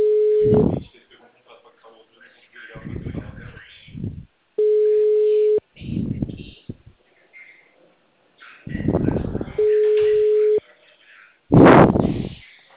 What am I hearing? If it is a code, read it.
Ringback tone (Europe, 425 Hz), with speech.
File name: out- -2011- - - .27236